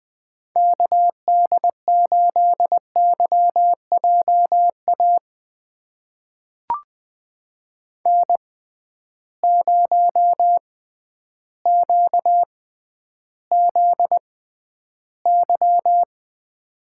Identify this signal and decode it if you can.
Morse code, heard as a keyed tone, about 20 wpm, spaced as for 11 wpm, 700 Hz. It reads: KD8YJA N0QZY
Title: USA Call Signs - 6x Character Spacing - 20wpm